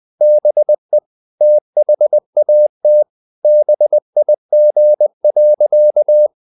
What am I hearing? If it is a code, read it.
Morse code, heard as a keyed tone, about 20 wpm, 600 Hz. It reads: BE THAT BIG.
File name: the old man and the sea part 2 20wpm